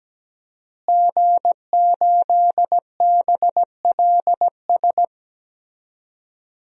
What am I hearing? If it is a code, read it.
Morse code, heard as a keyed tone, about 17 wpm, 700 Hz. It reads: G8BLS
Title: DXCC-100 Call Signs - 5 Characters - Repeated 2x - 17wpm